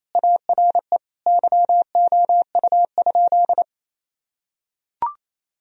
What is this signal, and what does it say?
Morse code, heard as a keyed tone, about 27 wpm, 700 Hz. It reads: ARE YOU?
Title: Sentences from Top 1000 Words - Review Part 3 - 28wpm